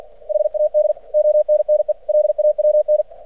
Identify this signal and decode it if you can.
Morse code, heard as a keyed tone, about 49 wpm, 600 Hz. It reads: HAD ONNE LAWN
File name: archived / CRSnet